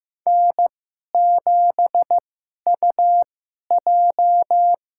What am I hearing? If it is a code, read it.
Morse code, heard as a keyed tone, about 15 wpm, spaced as for 11 wpm, 700 Hz. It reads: N7UJ
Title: USA Call Signs - 2x Character Spacing - 15wpm